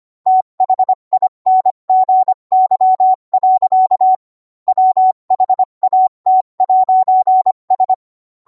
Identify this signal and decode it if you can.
Morse code, heard as a keyed tone, about 25 wpm, 750 Hz. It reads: THINGY. WHAT'S